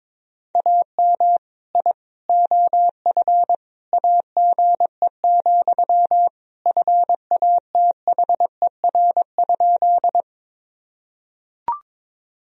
Morse code, heard as a keyed tone, about 21 wpm, 700 Hz. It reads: AM I OF AGE, FATHER?
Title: Sentences from Top 1000 Words - Review Part 3 - 22wpm